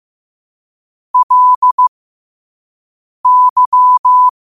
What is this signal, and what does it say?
Morse code, heard as a keyed tone, about 15 wpm, spaced as for 5 wpm, 1000 Hz. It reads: LY
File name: qotdfiles / 5wpm